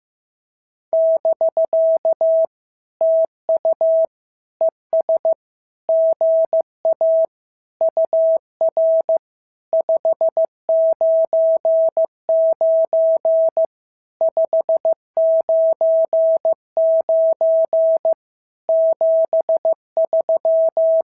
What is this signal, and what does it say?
Morse code, heard as a keyed tone, about 15 wpm, 650 Hz. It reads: <BK> TU ES GA UR 599 599 73
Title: SOTA QSO Practice 15wpm